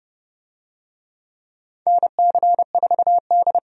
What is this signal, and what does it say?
Morse code, heard as a keyed tone, about 30 wpm, 700 Hz. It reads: NC4D
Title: USA Call Signs - 4 Characters - 30wpm